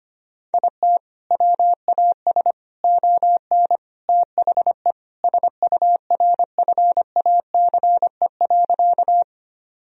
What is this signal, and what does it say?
Morse code, heard as a keyed tone, about 25 wpm, 700 Hz. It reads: IT WAS ON THE SURFACE.
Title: Sentences from Top 1000 Words - Review Part 3 - 25wpm